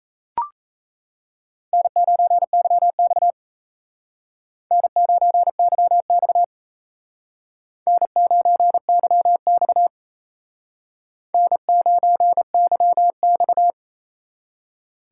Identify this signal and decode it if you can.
Morse code, heard as a keyed tone, about 35 wpm, 700 Hz. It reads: N9YX N9YX N9YX N9YX